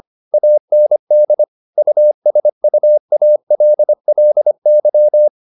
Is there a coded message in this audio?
Yes: Morse code from 0.3 to 5.3 s, heard as a keyed tone, about 25 wpm, 600 Hz. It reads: AND USUALLY